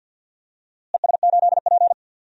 Morse code, heard as a keyed tone, about 47 wpm, 700 Hz. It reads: ES8P
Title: DXCC-100 Call Signs - 4 Characters - Repeated 3x - 50wpm